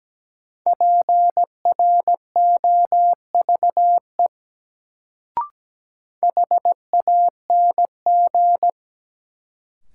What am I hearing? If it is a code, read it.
Morse code, heard as a keyed tone, about 17 wpm, 700 Hz. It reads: PROVE HANG
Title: Sentences from Top 1600 Words - New Word Drill Alt - 17wpm